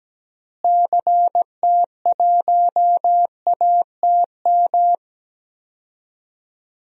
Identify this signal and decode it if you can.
Morse code, heard as a keyed tone, about 17 wpm, 700 Hz. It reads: CT1ATM
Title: DXCC-100 Call Signs - 6 Characters - Repeated 2x - 17wpm